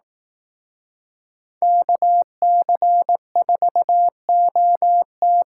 Morse code, heard as a keyed tone, about 18 wpm, 700 Hz. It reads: KC4OT
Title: Speed Racing - USA 2x2 Call Signs 15wpm